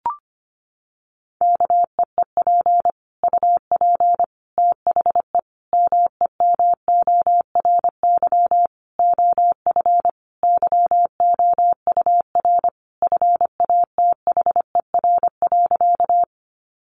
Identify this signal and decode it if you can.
Morse code, heard as a keyed tone, about 25 wpm, 700 Hz. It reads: KEEP UP THE MEMORY OF YOUR FATHER.
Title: Sentences from Top 1300 Words 25wpm